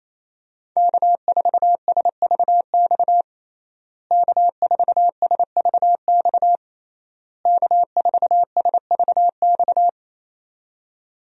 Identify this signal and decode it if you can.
Morse code, heard as a keyed tone, about 27 wpm, 700 Hz. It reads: K4SVX K4SVX K4SVX
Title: USA Call Signs - 5 Characters - Repeated 3x - 28wpm